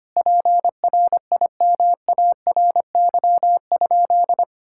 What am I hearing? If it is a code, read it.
Morse code, heard as a keyed tone, about 25 wpm, 700 Hz. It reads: PRIMARY?
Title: Sentences from Top 1000 Words - Review Part 5 - 25wpm